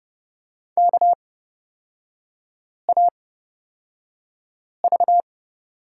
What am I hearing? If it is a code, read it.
Morse code, heard as a keyed tone, about 30 wpm, spaced as for 5 wpm, 700 Hz. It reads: KAV
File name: Lesson 36e - BT - Three Characters 30wpm